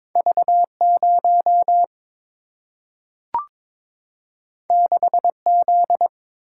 Morse code, heard as a keyed tone, about 22 wpm, 690 Hz. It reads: V0 6Z
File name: practice / 22wpm / Two Letter-Number Combinations - 22wpm